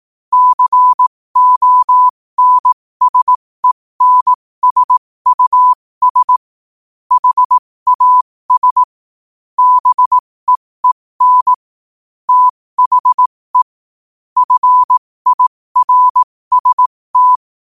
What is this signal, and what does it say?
Morse code, heard as a keyed tone, about 18 wpm, spaced as for 15 wpm, 1000 Hz. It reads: CONSENSUS HAS BEEN THE FIRST
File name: qotdfiles / 15wpm